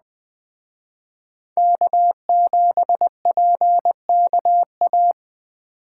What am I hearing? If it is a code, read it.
Morse code, heard as a keyed tone, about 20 wpm, 700 Hz. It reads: K7PKA